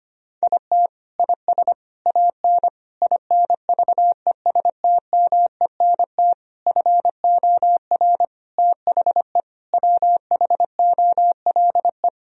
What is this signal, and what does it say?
Morse code, heard as a keyed tone, about 25 wpm, 700 Hz. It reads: IT IS AN INVESTMENT FOR THE WHOLE
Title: Sentences from Top 1000 Words - Review Part 4 - 25wpm